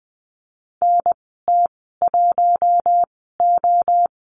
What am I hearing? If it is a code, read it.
Morse code, heard as a keyed tone, about 20 wpm, spaced as for 15 wpm, 700 Hz. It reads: NT1O
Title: 042 Lesson 8g - Number 1 - Call Signs 20wpm